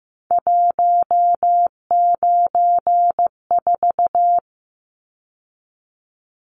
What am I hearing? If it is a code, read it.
Morse code, heard as a keyed tone, about 15 wpm, 700 Hz. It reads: 194
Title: Numbers - 3-Digit 15wpm